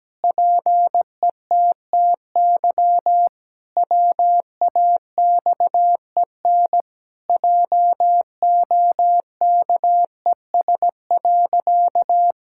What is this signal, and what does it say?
Morse code, heard as a keyed tone, about 17 wpm, 700 Hz. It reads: PETTY WAXEN JOKES.
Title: Pangram Challenge v2 - 17wpm